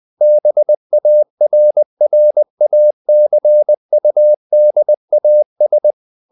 Morse code, heard as a keyed tone, about 20 wpm, 600 Hz. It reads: BARRACUDAS
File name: the old man and the sea part 2 20wpm